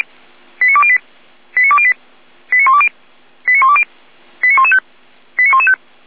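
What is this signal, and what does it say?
Five-tone selective call (ZVEI), heard as quick runs of five tones, tones of about 70 ms. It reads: 89208 89208 89120 89120 89115 89115
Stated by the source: ZVEI five-tone selective call 89025 received by SDR.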